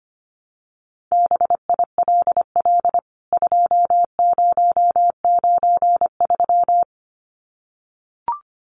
Morse code, heard as a keyed tone, about 25 wpm, 700 Hz. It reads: BILL 2093
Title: CWT Exchange 25wpm